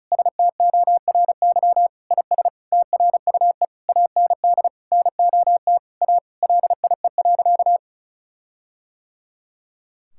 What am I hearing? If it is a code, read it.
Morse code, heard as a keyed tone, about 36 wpm, 700 Hz. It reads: STORY IS TRUE AND NOT A LIE.
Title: Sentences from Top 1600 Words 35wpm